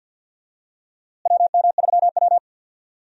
Morse code, heard as a keyed tone, about 51 wpm, 700 Hz. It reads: WM3W
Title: USA Call Signs - 4 Characters - Repeated 2x - 50wpm